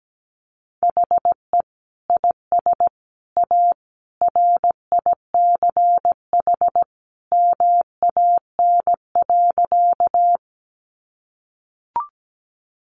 Morse code, heard as a keyed tone, about 17 wpm, 700 Hz. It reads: HE IS A RICH MAN.
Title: Sentences from Top 1100 Words 17wpm